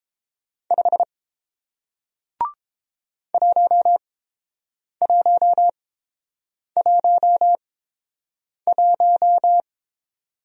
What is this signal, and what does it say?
Morse code, heard as a keyed tone, about 29 wpm, spaced as for 7 wpm, 700 Hz. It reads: 5 1111